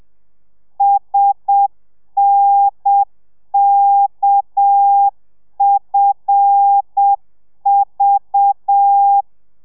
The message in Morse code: SNKFV